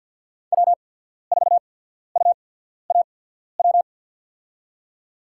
Morse code, heard as a keyed tone, about 54 wpm, spaced as for 14 wpm, 700 Hz. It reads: W4UAW
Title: USA Call Signs - 5 Characters - 8x Character Spacing - 50wpm